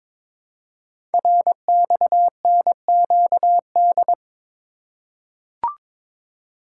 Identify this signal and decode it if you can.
Morse code, heard as a keyed tone, about 22 wpm, 700 Hz. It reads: RXNQD
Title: Code Groups - 5 Letters - 22wpm